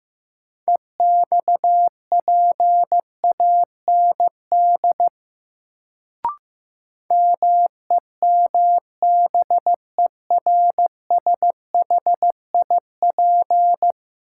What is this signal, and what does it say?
Morse code, heard as a keyed tone, about 15 wpm, 700 Hz. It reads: EXPAND MEMBERSHIP
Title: Sentences from Top 1400 Words - New Word Drill - 15wpm